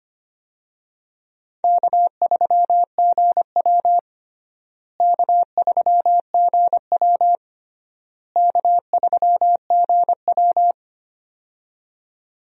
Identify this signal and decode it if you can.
Morse code, heard as a keyed tone, about 25 wpm, 700 Hz. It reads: K3GW K3GW K3GW
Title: USA Call Signs - 4 Characters - Repeated 3x - 25wpm